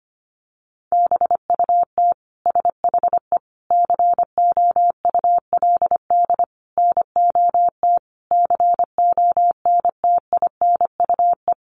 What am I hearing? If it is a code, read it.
Morse code, heard as a keyed tone, about 25 wpm, 700 Hz. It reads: BUT SHE COULD NOT CONTINUE
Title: Sentences from Top 1200 Words 25wpm